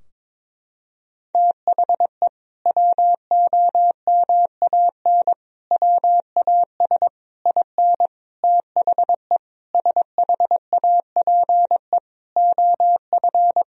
Morse code, heard as a keyed tone, about 22 wpm, 700 Hz. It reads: THE WOMAN WAS IN THE SHAPE OF
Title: Sentences from Top 1000 Words - Review Part 3 - 22wpm